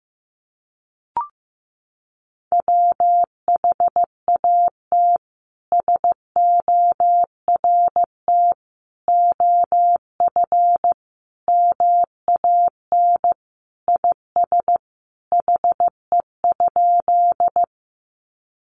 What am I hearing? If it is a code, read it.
Morse code, heard as a keyed tone, about 15 wpm, 700 Hz. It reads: WHAT SORT OF MAN IS HE?
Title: Sentences from Top 1300 Words 15wpm